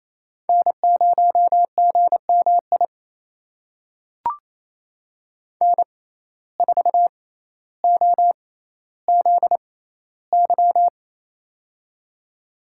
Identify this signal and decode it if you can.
Morse code, heard as a keyed tone, about 28 wpm, spaced as for 14 wpm, 700 Hz. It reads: N0GMI N4OZY
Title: USA Call Signs - 5 Characters - 6x Character Spacing - 28wpm